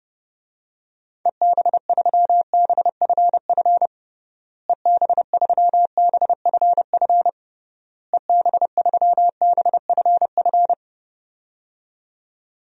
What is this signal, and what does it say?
Morse code, heard as a keyed tone, about 30 wpm, 700 Hz. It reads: EB3BFF EB3BFF EB3BFF
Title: DXCC-100 Call Signs - 6 Characters - Repeated 3x - 30wpm